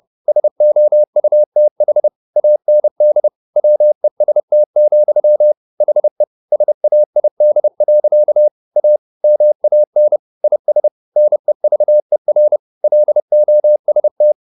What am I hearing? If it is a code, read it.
Morse code, heard as a keyed tone, about 30 wpm, 600 Hz. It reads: SOUTH AND WEST, HE SAID. A MAN IS NEVER LOST